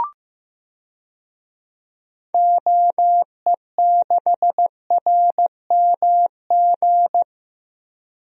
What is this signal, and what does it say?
Morse code, heard as a keyed tone, about 15 wpm, 700 Hz. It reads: OE6RMG
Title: DXCC-100 Call Signs - 6 Characters - Repeated 2x - 15wpm